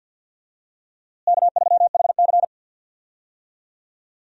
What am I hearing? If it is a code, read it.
Morse code, heard as a keyed tone, about 47 wpm, 700 Hz. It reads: K3HC
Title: USA Call Signs - 4 Characters - Repeated 3x - 50wpm